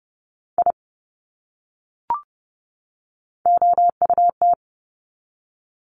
Morse code, heard as a keyed tone, about 30 wpm, 700 Hz. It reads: I OUT
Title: Top 200 Words 30wpm